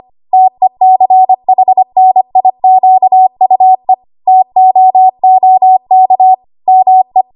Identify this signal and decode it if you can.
Morse code, heard as a keyed tone, about 25 wpm, 750 Hz. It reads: TECHNIQUE TOOK ME
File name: Archive / 25 WPM / 25WPM